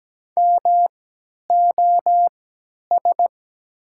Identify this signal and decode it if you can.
Morse code, heard as a keyed tone, about 17 wpm, spaced as for 10 wpm, 700 Hz. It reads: MOS
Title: 3x Character Spacing - Top 100 Words - 17wpm